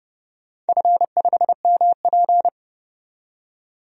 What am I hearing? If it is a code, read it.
Morse code, heard as a keyed tone, about 30 wpm, 700 Hz. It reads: F5MP
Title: DXCC-100 Call Signs - 4 Characters - 30wpm